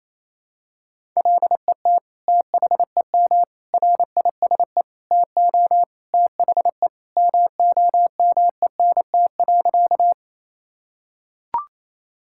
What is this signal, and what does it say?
Morse code, heard as a keyed tone, about 28 wpm, 700 Hz. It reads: LET THEM RISE TO THE MOMENT.
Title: Sentences from Top 1000 Words - Review Part 6 - 28wpm